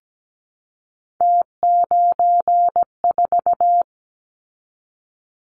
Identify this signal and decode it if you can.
Morse code, heard as a keyed tone, about 17 wpm, 700 Hz. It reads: T94
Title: Callsign Prefixes for Top 50 DXCC - 17wpm